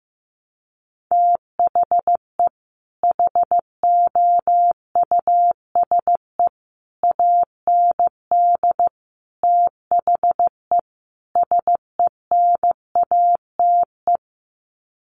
Morse code, heard as a keyed tone, about 15 wpm, 700 Hz. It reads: THE HOUSE AND THE SENATE
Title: Sets of 5 Words Encore2 15wpm